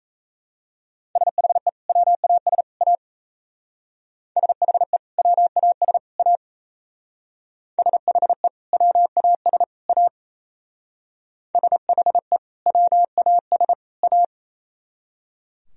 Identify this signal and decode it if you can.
Morse code, heard as a keyed tone, about 33 wpm, 700 Hz. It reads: SHE WAS A SHE WAS A SHE WAS A SHE WAS A
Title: Speed-Racing - Sets of 3 Words 28wpm